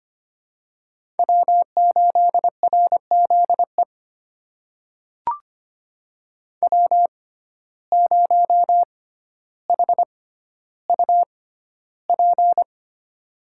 Morse code, heard as a keyed tone, about 25 wpm, spaced as for 13 wpm, 700 Hz. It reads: W8RZE W0HUP
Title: USA Call Signs - 5 Characters - 6x Character Spacing - 25wpm